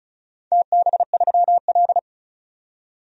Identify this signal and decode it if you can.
Morse code, heard as a keyed tone, about 36 wpm, 700 Hz. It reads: TB3L
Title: DXCC-100 Call Signs - 4 Characters - 35wpm